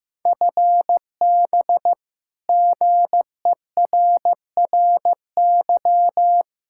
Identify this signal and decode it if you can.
Morse code, heard as a keyed tone, about 15 wpm, 700 Hz. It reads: FB GERRY
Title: Practice QSO - Exchange 3 - More Info - 15wpm